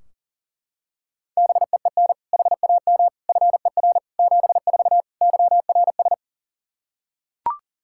Morse code, heard as a keyed tone, about 40 wpm, 700 Hz. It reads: BEEN HAM FER 74 YRS
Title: K1USN SST QSO Practice 40wpm